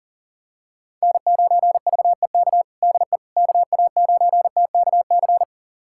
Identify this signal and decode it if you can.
Morse code, heard as a keyed tone, about 40 wpm, 700 Hz. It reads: N9VEK DE KA9TKC
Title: USA Callsign DE Callsign 40wpm